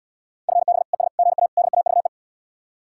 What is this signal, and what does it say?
Morse code, heard as a keyed tone, about 79 wpm, 700 Hz. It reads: FB ES GA DAVE